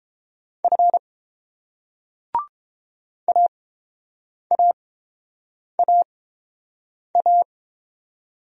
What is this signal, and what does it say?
Morse code, heard as a keyed tone, about 28 wpm, spaced as for 7 wpm, 690 Hz. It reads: F AAAA